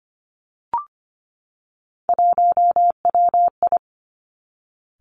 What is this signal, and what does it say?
Morse code, heard as a keyed tone, about 25 wpm, 700 Hz. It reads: 1WI
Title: Three Challenging Letter-Number Combinations 25wpm